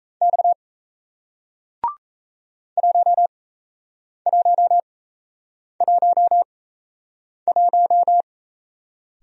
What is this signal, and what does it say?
Morse code, heard as a keyed tone, about 37 wpm, spaced as for 8 wpm, 700 Hz. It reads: X 1111